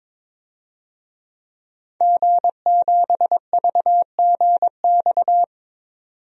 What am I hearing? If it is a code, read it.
Morse code, heard as a keyed tone, about 22 wpm, 700 Hz. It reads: G7VGX
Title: DXCC-100 Call Signs - 5 Characters - 22wpm